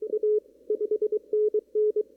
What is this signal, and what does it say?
Morse code, heard as a keyed tone, about 23 wpm, 420 Hz. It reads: A 5NN
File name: CW / kt5j wpx cw-op-k5pi